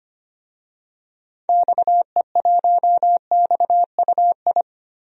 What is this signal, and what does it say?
Morse code, heard as a keyed tone, about 25 wpm, 700 Hz. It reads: XE1XUI
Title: DXCC-100 Call Signs - 6 Characters - 25wpm